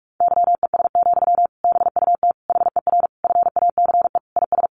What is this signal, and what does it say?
Morse code, heard as a keyed tone, about 44 wpm, 700 Hz. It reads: YES, BUT HER FACE IS
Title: Sentences from Top 1200 Words 45wpm